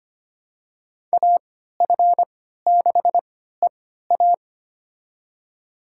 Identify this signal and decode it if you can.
Morse code, heard as a keyed tone, about 25 wpm, spaced as for 14 wpm, 700 Hz. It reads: AF6EA
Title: USA Call Signs - 5 Characters - 3x Character Spacing - 25wpm